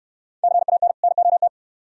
Morse code, heard as a keyed tone, about 66 wpm, 700 Hz. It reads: CAN NOT